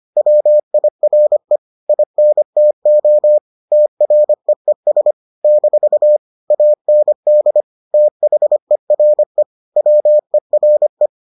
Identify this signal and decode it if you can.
Morse code, heard as a keyed tone, about 25 wpm, 600 Hz. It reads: WIRE INTO TREES - AND THERE WERE